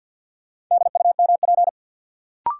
Morse code, heard as a keyed tone, about 52 wpm, 700 Hz. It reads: DUMP
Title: Sentences from Top 1300 Words - New Word Drill - 50wpm